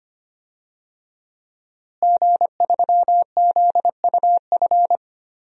Morse code, heard as a keyed tone, about 25 wpm, 700 Hz. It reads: G3ZUF
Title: DXCC-100 Call Signs - 5 Characters - Repeated 2x - 25wpm